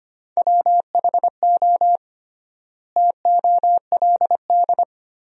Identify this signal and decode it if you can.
Morse code, heard as a keyed tone, about 25 wpm, 700 Hz. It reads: WHO TOLD
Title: 3x Word-Spacing - Sentences from Top 400 Words - 25wpm